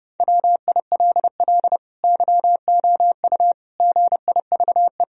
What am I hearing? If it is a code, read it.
Morse code, heard as a keyed tone, about 30 wpm, 700 Hz. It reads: WILL YOU GIVE
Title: Sentences from Top 1000 Words - Review Part 3 - 30wpm